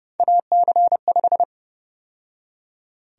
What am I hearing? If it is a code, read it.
Morse code, heard as a keyed tone, about 30 wpm, 700 Hz. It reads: AC5
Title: USA Call Sign Prefixes - 30wpm